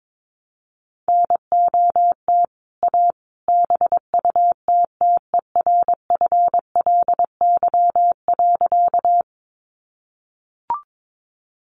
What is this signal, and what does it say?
Morse code, heard as a keyed tone, about 22 wpm, 700 Hz. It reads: NOT A BUTTERFLY.
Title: A Sound Of Thunder 22wpm